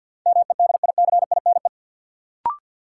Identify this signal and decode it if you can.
Morse code, heard as a keyed tone, about 51 wpm, 700 Hz. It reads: MEDICINE
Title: Sentences from Top 900 Words - New Word Drill - 50wpm